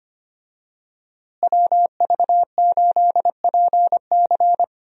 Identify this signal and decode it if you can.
Morse code, heard as a keyed tone, about 25 wpm, 700 Hz. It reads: WV8PC
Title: USA Call Signs - 5 Characters - Repeated 3x - 25wpm